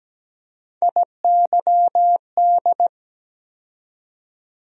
Morse code, heard as a keyed tone, about 17 wpm, 700 Hz. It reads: IYD
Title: USA Call Sign Postfixes - 3 Letters 17wpm